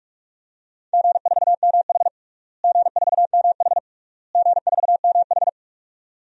Morse code, heard as a keyed tone, about 46 wpm, 700 Hz. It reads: G4MH G4MH G4MH
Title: DXCC-100 Call Signs - 4 Characters - Repeated 3x - 45wpm